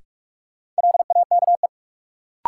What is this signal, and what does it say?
Morse code, heard as a keyed tone, about 42 wpm, 700 Hz. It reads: LAKE
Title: Sentences from Top 1100 Words - New Word Drill Alt - 45wpm